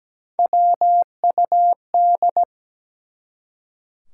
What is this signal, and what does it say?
Morse code, heard as a keyed tone, about 17 wpm, 700 Hz. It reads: WUD
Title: Top 180 QSO Elements 17wpm